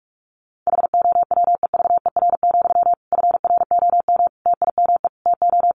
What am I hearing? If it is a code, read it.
Morse code, heard as a keyed tone, about 44 wpm, 700 Hz. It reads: HOWEVER, FROM TIME TO
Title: Sentences from Top 1300 Words 45wpm